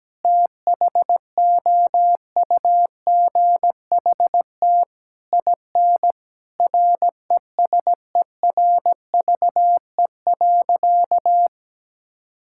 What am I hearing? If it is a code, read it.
Morse code, heard as a keyed tone, about 17 wpm, 700 Hz. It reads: THOUGHT IN RESERVE.
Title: Sentences from Top 1000 Words - Review Part 6 - 17wpm